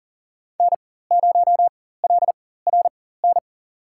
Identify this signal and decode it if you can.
Morse code, heard as a keyed tone, about 40 wpm, spaced as for 19 wpm, 700 Hz. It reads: N0LRN
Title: USA Call Signs - 5 Characters - 4x Character Spacing - 40wpm